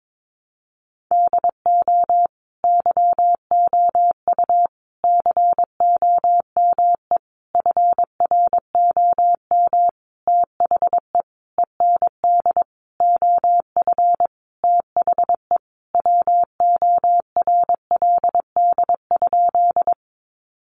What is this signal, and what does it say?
Morse code, heard as a keyed tone, about 22 wpm, 700 Hz. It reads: DO YOU COME FROM THE END OF THE WORLD?
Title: Sentences from Top 200 Words 22wpm